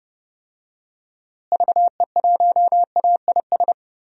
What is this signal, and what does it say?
Morse code, heard as a keyed tone, about 30 wpm, 700 Hz. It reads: VE1AIS
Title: DXCC-100 Call Signs - 6 Characters - Repeated 2x - 30wpm